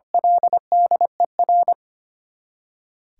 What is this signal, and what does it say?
Morse code, heard as a keyed tone, about 25 wpm, 700 Hz. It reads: LDER
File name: Eight Letter Words - 25wpm